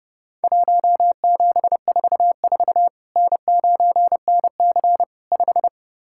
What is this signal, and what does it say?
Morse code, heard as a keyed tone, about 32 wpm, 700 Hz. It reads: 1744 N9NC 5
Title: Worked All Europe DX Contest - QTC Exchange - 30wpm